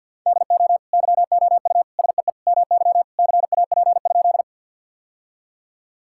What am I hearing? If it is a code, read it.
Morse code, heard as a keyed tone, about 50 wpm, 700 Hz. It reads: DO YOU SEE MY CAP?
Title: Sentences from Top 1600 Words 50wpm